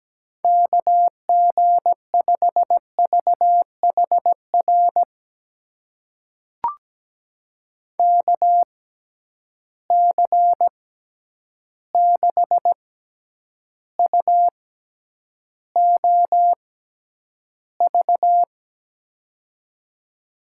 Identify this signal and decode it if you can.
Morse code, heard as a keyed tone, about 17 wpm, spaced as for 9 wpm, 700 Hz. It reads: KG5VHR KC6UOV
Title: USA Call Signs - 6 Characters - 6x Character Spacing - 17wpm